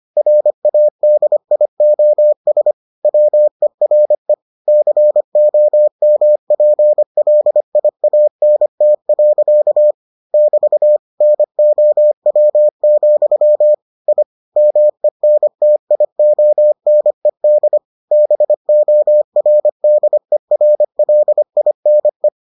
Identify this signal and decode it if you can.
Morse code, heard as a keyed tone, about 25 wpm, 600 Hz. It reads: RADIOS WERE COMPLIANT. <BT> NOW, I MENTIONED BORDERLINE